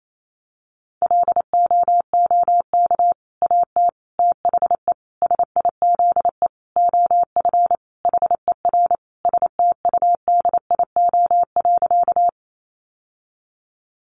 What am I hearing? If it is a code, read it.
Morse code, heard as a keyed tone, about 28 wpm, 700 Hz. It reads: LOOK AT THE SIZE OF HER STUDIO.
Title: Sentences from Top 1200 Words 28wpm